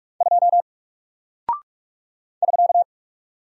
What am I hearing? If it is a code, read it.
Morse code, heard as a keyed tone, about 48 wpm, 700 Hz. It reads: 2 <SK>
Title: Single Character (Letter:Number:Punctuation:Prosign) Alt 45wpm